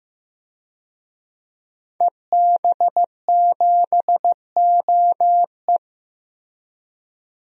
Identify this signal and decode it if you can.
Morse code, heard as a keyed tone, about 15 wpm, 700 Hz. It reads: EB7OE
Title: DXCC-100 Call Signs - 5 Characters - Repeated 2x - 15wpm